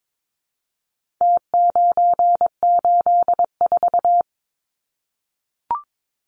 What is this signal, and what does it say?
Morse code, heard as a keyed tone, about 22 wpm, 700 Hz. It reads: T984
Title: Numbers - Contest Style - 4-Digit 22wpm